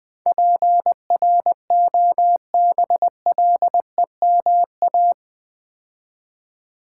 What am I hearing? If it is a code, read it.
Morse code, heard as a keyed tone, about 21 wpm, 700 Hz. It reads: PROBLEMA
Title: Build A Word - 4 Letters At A Time - 11-Letter Words - 20wpm